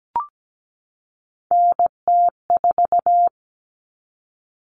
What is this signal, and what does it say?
Morse code, heard as a keyed tone, about 17 wpm, 700 Hz. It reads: NT4
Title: All Cut Numbers - 3-Digit 17wpm